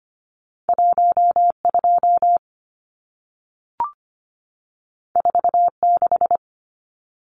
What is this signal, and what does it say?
Morse code, heard as a keyed tone, about 25 wpm, 700 Hz. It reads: 12 46
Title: Numbers - 2-Digit 25wpm